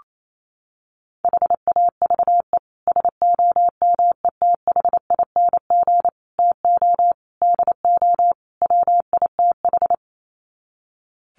Morse code, heard as a keyed tone, about 28 wpm, 700 Hz. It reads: HAVE SOMETHING TO DO WITH
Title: Sets of 5 Words Encore 28wpm